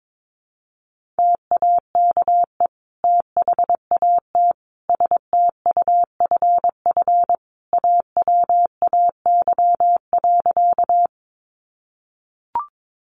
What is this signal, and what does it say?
Morse code, heard as a keyed tone, about 22 wpm, 700 Hz. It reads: TAKE THAT STUFF AWAY.
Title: Sentences from Top 1400 Words 22wpm